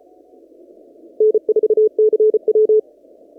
Morse code, heard as a keyed tone, about 33 wpm, 430 Hz. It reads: N4CW
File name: CW / kt5j wpx cw-op-k5pi